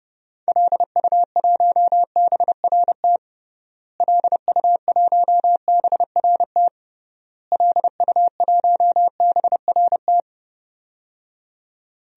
Morse code, heard as a keyed tone, about 32 wpm, 700 Hz. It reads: LU1BRT LU1BRT LU1BRT
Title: DXCC-100 Call Signs - 6 Characters - Repeated 3x - 30wpm